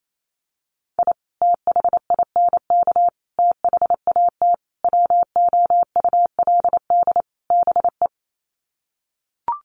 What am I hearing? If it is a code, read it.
Morse code, heard as a keyed tone, about 28 wpm, 700 Hz. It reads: I THINK THAT WOULD BE
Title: Sets of 5 Words Encore2 28wpm